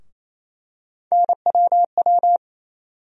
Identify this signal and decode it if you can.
Morse code, heard as a keyed tone, about 27 wpm, 700 Hz. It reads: NWW
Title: USA Call Sign Postfixes - 3 Letters 28wpm